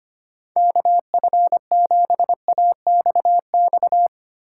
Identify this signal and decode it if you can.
Morse code, heard as a keyed tone, about 25 wpm, 700 Hz. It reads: KF7AXX
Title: USA Call Signs - 6 Characters - 8x Character Spacing - 25wpm